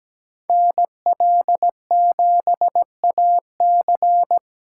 Morse code, heard as a keyed tone, about 20 wpm, 700 Hz. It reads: NL7AC
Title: Speed Racing - USA 2x2 Call Signs 17wpm